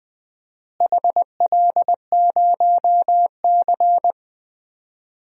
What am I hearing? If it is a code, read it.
Morse code, heard as a keyed tone, about 20 wpm, 700 Hz. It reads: HL0C
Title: DXCC-100 Call Signs - 4 Characters - Repeated 2x - 20wpm